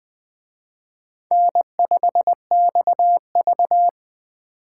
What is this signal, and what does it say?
Morse code, heard as a keyed tone, about 20 wpm, 700 Hz. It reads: N5XV
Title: Speed Racing - USA Call Signs 17wpm